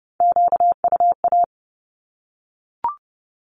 Morse code, heard as a keyed tone, about 30 wpm, 700 Hz. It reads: QUA
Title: Q-Codes 30wpm